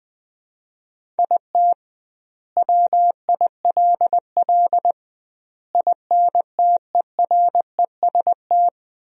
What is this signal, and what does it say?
Morse code, heard as a keyed tone, about 20 wpm, 700 Hz. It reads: IT WILL INTEREST
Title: 2x Word-Spacing - Sentences from Top 400 Words - 20wpm